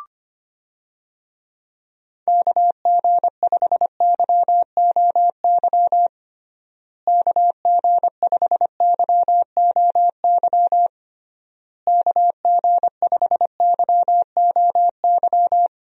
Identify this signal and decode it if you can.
Morse code, heard as a keyed tone, about 25 wpm, 700 Hz. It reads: KG5YOY KG5YOY KG5YOY
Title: USA Call Signs - 6 Characters - Repeated 3x - 25wpm